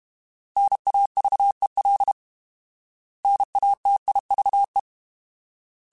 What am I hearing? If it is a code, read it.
Morse code, heard as a keyed tone, about 32 wpm, 800 Hz. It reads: NAVEL NATIVE
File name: Alphabetical / Alphabetical mp3s / Ns @30